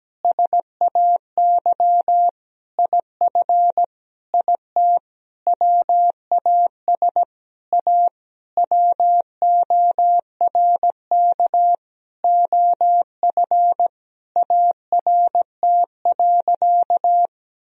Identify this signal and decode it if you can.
Morse code, heard as a keyed tone, about 17 wpm, 700 Hz. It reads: SAY IF IT WAS A WORK OF ART.
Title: Sentences from Top 1000 Words - Review Part 3 - 17wpm